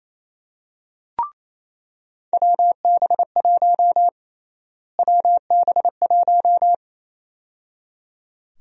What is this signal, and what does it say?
Morse code, heard as a keyed tone, about 28 wpm, 700 Hz. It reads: WB1 WB1
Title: USA Call Sign Prefixes - Repeated 2x - 28wpm